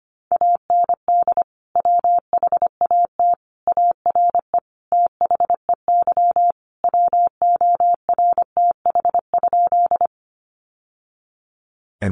Morse code, heard as a keyed tone, about 25 wpm, 700 Hz. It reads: AND WHAT ARE THEY WORTH?
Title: Sentences from Top 800 Words 25wpm